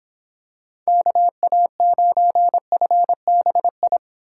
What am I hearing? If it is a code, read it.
Morse code, heard as a keyed tone, about 26 wpm, 700 Hz. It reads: KA9FBI